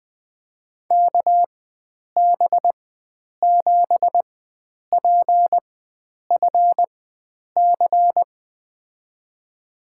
Morse code, heard as a keyed tone, about 20 wpm, spaced as for 9 wpm, 700 Hz. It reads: KB7PFC
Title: USA Call Signs - 6 Characters - 4x Character Spacing - 20wpm